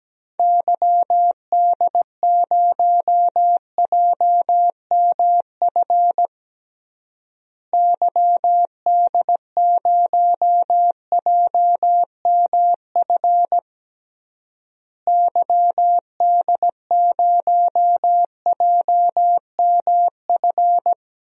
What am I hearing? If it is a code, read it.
Morse code, heard as a keyed tone, about 17 wpm, 700 Hz. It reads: YD0JMF YD0JMF YD0JMF
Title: DXCC-100 Call Signs - 6 Characters - Repeated 3x - 17wpm